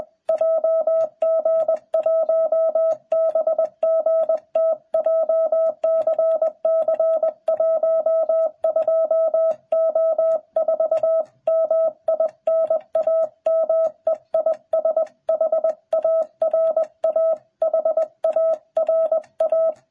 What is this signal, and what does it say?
Morse code, heard as a keyed tone, about 21 wpm, 660 Hz. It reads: JG1BGT JCC 12O4 MI NAME IS HARA HARA